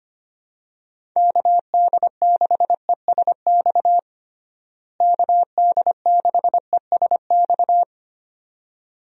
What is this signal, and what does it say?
Morse code, heard as a keyed tone, about 25 wpm, 700 Hz. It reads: KD6ESX KD6ESX
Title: USA Call Signs - 6 Characters - Repeated 2x - 25wpm